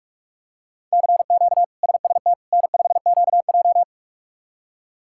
Morse code, heard as a keyed tone, about 42 wpm, 700 Hz. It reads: CQ SST N5QJ
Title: K1USN SST QSO Practice - 45wpm